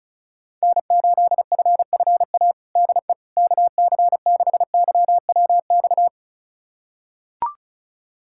Morse code, heard as a keyed tone, about 36 wpm, 700 Hz. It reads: N8FFA DE KC6YWX
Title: USA Callsign DE Callsign 35wpm